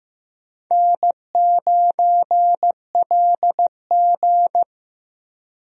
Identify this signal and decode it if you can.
Morse code, heard as a keyed tone, about 16 wpm, 700 Hz. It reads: N9LG
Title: USA Call Signs - 4 Characters - Repeated 2x - 15wpm